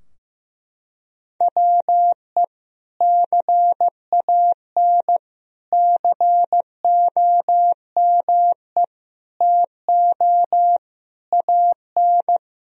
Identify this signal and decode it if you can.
Morse code, heard as a keyed tone, about 15 wpm, 700 Hz. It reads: WE CAN COME TO AN
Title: Sentences from Top 1000 Words - Review Part 3 - 15wpm